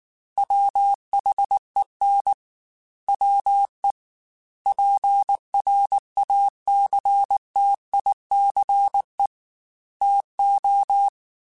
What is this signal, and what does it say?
Morse code, heard as a keyed tone, about 19 wpm, 800 Hz. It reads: WHEN WE PRACTICE TO
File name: Quotes / Quotes-mp3s / Deceive @18